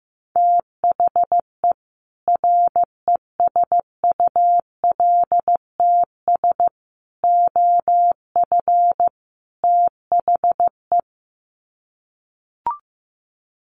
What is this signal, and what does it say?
Morse code, heard as a keyed tone, about 15 wpm, 700 Hz. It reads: THE RESULTS OF THE
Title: Sets of 4 Words 15wpm